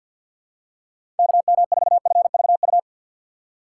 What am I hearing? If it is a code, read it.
Morse code, heard as a keyed tone, about 53 wpm, 700 Hz. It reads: KM4FVU
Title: USA Call Signs - 4x Character Spacing - 50wpm